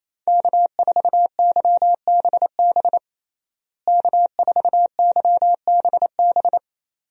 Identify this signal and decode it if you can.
Morse code, heard as a keyed tone, about 28 wpm, 700 Hz. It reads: K4YBB K4YBB
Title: USA Call Signs - 5 Characters - Repeated 2x - 28wpm